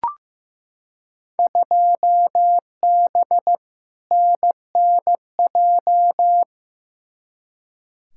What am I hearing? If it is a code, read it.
Morse code, heard as a keyed tone, about 15 wpm, 700 Hz. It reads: 2B NNJ
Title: Field Day Exchanges Encore - 15wpm